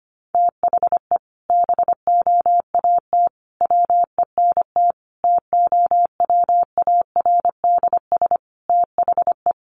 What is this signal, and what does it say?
Morse code, heard as a keyed tone, about 25 wpm, 700 Hz. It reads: THE BOAT WENT TOWARDS THE
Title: Sentences from Top 1100 Words 25wpm